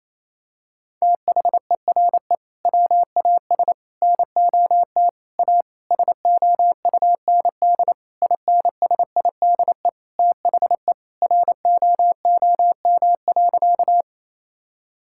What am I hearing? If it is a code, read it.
Morse code, heard as a keyed tone, about 28 wpm, 700 Hz. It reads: THERE WAS NOT A SOUND INSIDE THE ROOM.
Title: Sentences from Top 1000 Words - Review Part 6 - 28wpm